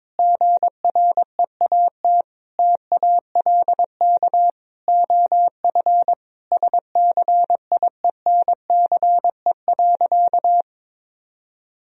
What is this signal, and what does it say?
Morse code, heard as a keyed tone, about 22 wpm, 700 Hz. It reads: GREAT TALK OF SCIENCE.
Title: Sentences from Top 1000 Words - Review Part 2 - 22wpm